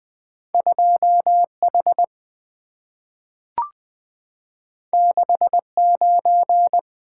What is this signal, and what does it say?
Morse code, heard as a keyed tone, about 20 wpm, 700 Hz. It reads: 2 H 6 9
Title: Two Letter-Number Combinations - 20wpm